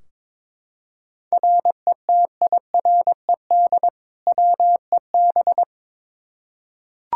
Morse code, heard as a keyed tone, about 22 wpm, 700 Hz. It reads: RETIRED WEB
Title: Practice QSO - Exchange 3 - More Info - 22wpm